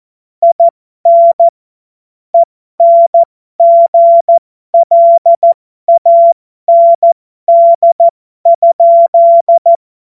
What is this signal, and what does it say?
Morse code, heard as a keyed tone, about 14 wpm, spaced as for 12 wpm, 680 Hz. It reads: IN ENGLAND?
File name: Morsekurs / Witze